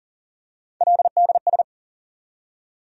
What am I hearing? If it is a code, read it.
Morse code, heard as a keyed tone, about 40 wpm, 700 Hz. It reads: LDS
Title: USA Call Sign Postfixes - 3 Letters 40wpm